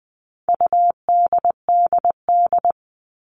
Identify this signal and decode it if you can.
Morse code, heard as a keyed tone, about 20 wpm, 700 Hz. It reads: UDDD
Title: All Cut Numbers - 4-Digit 20wpm